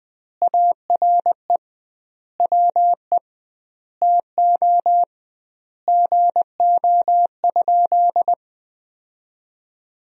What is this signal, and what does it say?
Morse code, heard as a keyed tone, about 20 wpm, 700 Hz. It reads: ARE WE TO GO?
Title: 2x Word-Spacing - Sentences from Top 200 Words - 20wpm